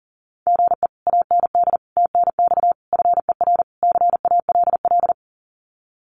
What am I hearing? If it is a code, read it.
Morse code, heard as a keyed tone, about 40 wpm, 700 Hz. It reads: GE AND TNX FER CALL
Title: Common QSO Phrases Encore 40wpm